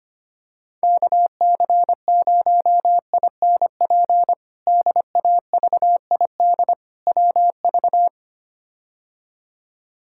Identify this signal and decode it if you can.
Morse code, heard as a keyed tone, about 25 wpm, 700 Hz. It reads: KC0INP DAVID WV
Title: K1USN SST QSO Practice - 25wpm